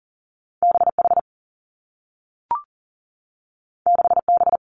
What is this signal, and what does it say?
Morse code, heard as a keyed tone, about 38 wpm, 700 Hz. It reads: B H 6 B
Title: Two Challenging Letter-Number Combinations 40wpm